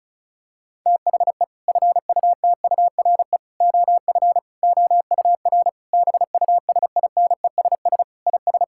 Morse code, heard as a keyed tone, about 36 wpm, 700 Hz. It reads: THE FUTURE OF OUR BUSINESS IS
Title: Sentences from Top 1000 Words - Review Part 6 - 35wpm